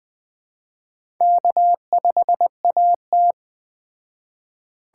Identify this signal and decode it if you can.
Morse code, heard as a keyed tone, about 20 wpm, 700 Hz. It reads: K5AT
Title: Speed Racing - USA 1x2 Call Signs 17wpm